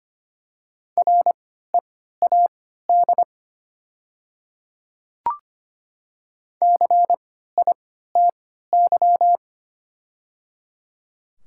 Morse code, heard as a keyed tone, about 25 wpm, spaced as for 14 wpm, 700 Hz. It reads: READ CITY
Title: 3x Character Spacing - Top 300 Words - 25wpm